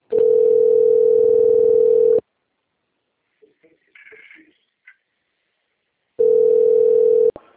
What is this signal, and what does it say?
Ringback tone (North America), 440 + 480 Hz.